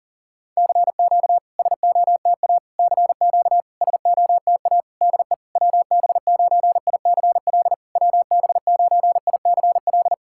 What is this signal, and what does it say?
Morse code, heard as a keyed tone, about 40 wpm, 690 Hz. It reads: CQ SOTA CQ SOTA DE WB9ICL WB9ICL
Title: SOTA QSO Practice 40wpm